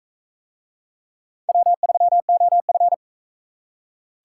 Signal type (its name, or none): Morse code